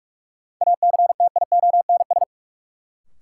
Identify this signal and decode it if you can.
Morse code, heard as a keyed tone, about 46 wpm, 700 Hz. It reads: ACTIONS
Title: Sentences from Top 1000 Words - Review Part 6 - New Word Drill - 45wpm